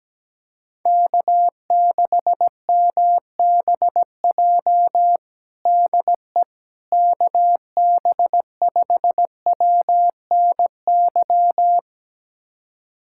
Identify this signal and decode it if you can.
Morse code, heard as a keyed tone, about 17 wpm, 700 Hz. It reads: K6MBJ DE KB5WNY
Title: USA Callsign DE Callsign 17wpm